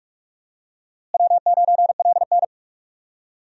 Morse code, heard as a keyed tone, about 44 wpm, 700 Hz. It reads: W9LN
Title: USA Call Signs - 4 Characters - Repeated 3x - 45wpm